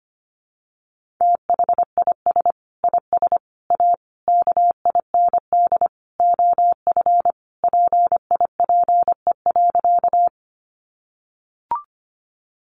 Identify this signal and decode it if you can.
Morse code, heard as a keyed tone, about 25 wpm, 700 Hz. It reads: THIS IS A KIND OF PIPE.
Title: Sentences from Top 1500 Words 25wpm